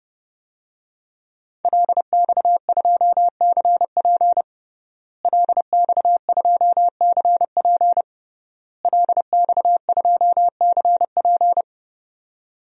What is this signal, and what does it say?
Morse code, heard as a keyed tone, about 30 wpm, 700 Hz. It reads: LX2CP LX2CP LX2CP
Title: DXCC-100 Call Signs - 5 Characters - Repeated 3x - 30wpm